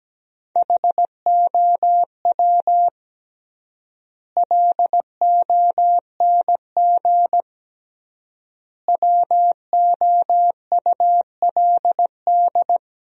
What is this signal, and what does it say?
Morse code, heard as a keyed tone, about 17 wpm, 700 Hz. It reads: HOW LONG WOULD
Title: 3x Word-Spacing - Sentences from Top 500 Words - 17wpm